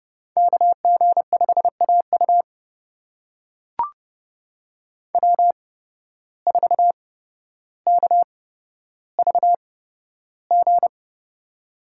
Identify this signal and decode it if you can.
Morse code, heard as a keyed tone, about 31 wpm, spaced as for 13 wpm, 700 Hz. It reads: KG5AU W4KVG
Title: USA Call Signs - 5 Characters - 8x Character Spacing - 30wpm